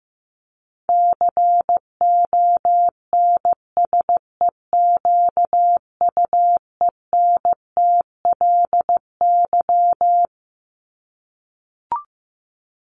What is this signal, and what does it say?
Morse code, heard as a keyed tone, about 15 wpm, 700 Hz. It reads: CONSEQUENTLY
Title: LY Suffix - Top 100 Words with Base Words 15wpm